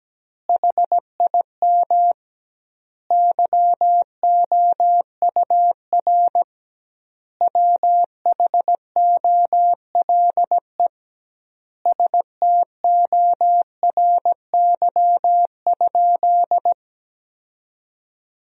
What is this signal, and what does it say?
Morse code, heard as a keyed tone, about 17 wpm, 700 Hz. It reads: HIM YOUR WHOLE STORY?
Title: 2x Word-Spacing - Sentences from Top 400 Words - 17wpm